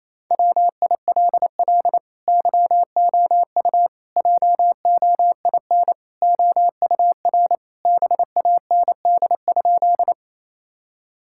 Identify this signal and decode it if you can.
Morse code, heard as a keyed tone, about 28 wpm, 700 Hz. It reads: WILL YOU JOIN OUR BAND?
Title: Sentences from Top 1600 Words 28wpm